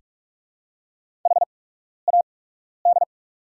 Morse code, heard as a keyed tone, about 48 wpm, spaced as for 12 wpm, 700 Hz. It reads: HAD
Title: 8x Character Spacing - Top 100 Words - 45wpm